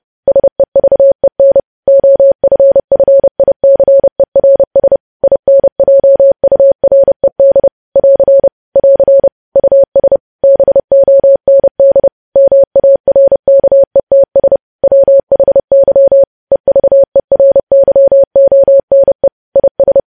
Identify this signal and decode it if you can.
Morse code, heard as a keyed tone, about 30 wpm, 550 Hz. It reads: SEVEN OFFICERS INJURED <AR> <AR> US BOND MARKETS WHY EVERYONE IS
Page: news headlines in morse code at 30 wpm